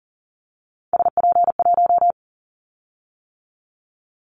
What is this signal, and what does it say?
Morse code, heard as a keyed tone, about 40 wpm, 700 Hz. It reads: SP1
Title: Callsign Prefixes for Top 25 DXCC - 40wpm